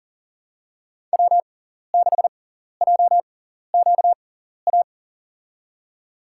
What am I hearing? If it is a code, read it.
Morse code, heard as a keyed tone, about 40 wpm, spaced as for 14 wpm, 700 Hz. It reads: W6JQA
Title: USA Call Signs - 5 Characters - 6x Character Spacing - 40wpm